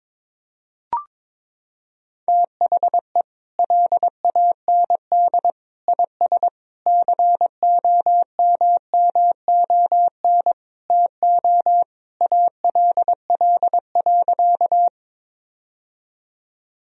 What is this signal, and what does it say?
Morse code, heard as a keyed tone, about 22 wpm, 700 Hz. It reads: THE LAND IS COMMON TO ALL.
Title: Sentences from Top 1000 Words - Review Part 3 - 22wpm